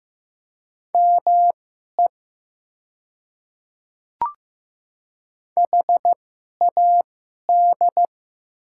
Morse code, heard as a keyed tone, about 15 wpm, spaced as for 11 wpm, 700 Hz. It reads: ME HAD
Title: 2x Character Spacing - Top 200 Words - 15wpm